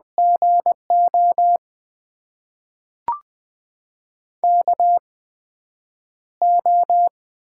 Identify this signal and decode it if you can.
Morse code, heard as a keyed tone, about 20 wpm, spaced as for 9 wpm, 700 Hz. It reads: GO KO
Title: USA Call Signs - 4 Characters - 8x Character Spacing - 20wpm